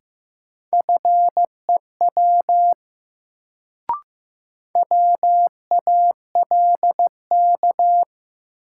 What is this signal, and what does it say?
Morse code, heard as a keyed tone, about 15 wpm, 700 Hz. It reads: FEW WALK
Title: Sentences from Top 300 Words - New Word Drill - 15wpm